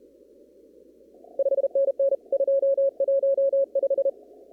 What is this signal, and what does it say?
Morse code, heard as a keyed tone, about 32 wpm, 550 Hz. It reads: 5NN 215
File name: CW / kt5j wpx cw-op-k5pi